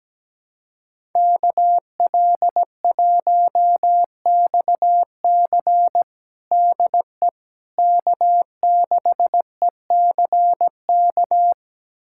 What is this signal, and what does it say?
Morse code, heard as a keyed tone, about 17 wpm, 700 Hz. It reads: KL1XC DE K6ECK
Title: USA Callsign DE Callsign - 5 Characters - 17wpm